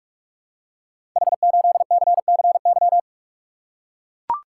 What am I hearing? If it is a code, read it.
Morse code, heard as a keyed tone, about 44 wpm, 700 Hz. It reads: H8CCY
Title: Code Groups - 5 Letter:Numbers - 45wpm